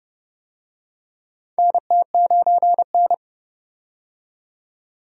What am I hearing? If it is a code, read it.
Morse code, heard as a keyed tone, about 31 wpm, 700 Hz. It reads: NT9N
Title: USA Call Signs - 4 Characters - 30wpm